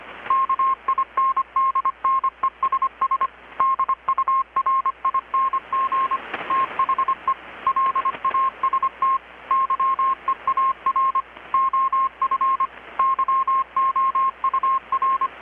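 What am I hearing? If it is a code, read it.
Morse code, heard as a keyed tone, about 25 wpm, 1070 Hz. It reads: KINDNESS DURING THE LAST YEAR OF YOUR
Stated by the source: Morse from KPH's Night of Nights XX maritime radiotelegraphy event, received in Chiapas, Mexico about 3,800 km away on a Kenwood R-600 with a grounded 10 m long-wire antenna.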